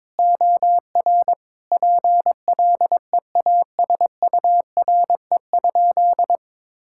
Morse code, heard as a keyed tone, about 22 wpm, 700 Hz. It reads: OR PLEASURE?
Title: Sentences from Top 1600 Words 22wpm